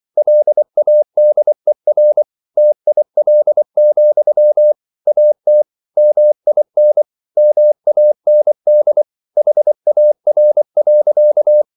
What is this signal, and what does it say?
Morse code, heard as a keyed tone, about 24 wpm, 600 Hz. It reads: LADER TIL, AT MIN MAND HAR.
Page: Ferie 23 wpm